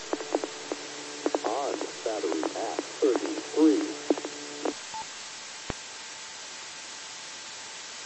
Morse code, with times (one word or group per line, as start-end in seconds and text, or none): none